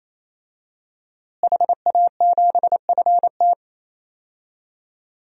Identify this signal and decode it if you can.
Morse code, heard as a keyed tone, about 28 wpm, 700 Hz. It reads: HA7FT
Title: DXCC-100 Call Signs - 5 Characters - Repeated 3x - 28wpm